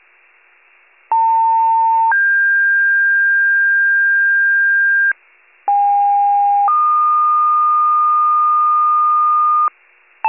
Two-tone page, heard as a one-second tone, then a three-second tone, about 900 then 1600 Hz, about 810 then 1220 Hz. It is heard from 1.1 to 9.7 s.